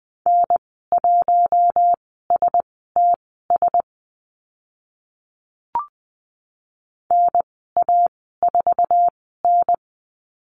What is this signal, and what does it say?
Morse code, heard as a keyed tone, about 20 wpm, spaced as for 15 wpm, 700 Hz. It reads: N1STS NA4N
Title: 055 Lesson 10g - Letter r - Call Signs 20wpm